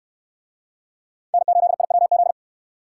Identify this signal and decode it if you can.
Morse code, heard as a keyed tone, about 67 wpm, 700 Hz. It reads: N8IWZ